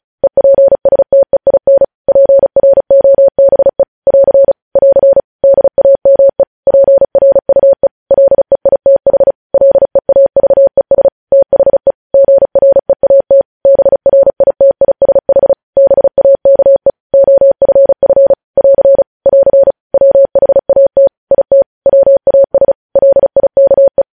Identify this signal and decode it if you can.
Morse code, heard as a keyed tone, about 36 wpm, 550 Hz. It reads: EPSTEIN PROBE <AR> <AR> DAME PRUE LEITH LEAVES THE GREAT BRITISH BAKE OFF <AR> <AR> WHAT IT WAS LIKE
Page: News Headlines in Morse Code at 35 WPM